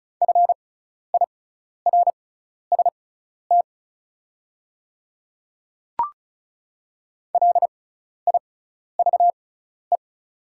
Morse code, heard as a keyed tone, about 34 wpm, spaced as for 12 wpm, 700 Hz. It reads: FIRST LIVE
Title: 6x Character Spacing - Top 200 Words - 35wpm